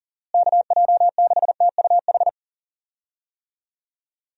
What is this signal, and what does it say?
Morse code, heard as a keyed tone, about 40 wpm, 700 Hz. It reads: KJ6TUH
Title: USA Call Signs - 6 Characters - Repeated 3x - 40wpm